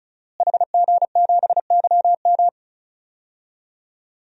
Morse code, heard as a keyed tone, about 35 wpm, 700 Hz. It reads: HG7YM
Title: DXCC-100 Call Signs - 5 Characters - Repeated 2x - 35wpm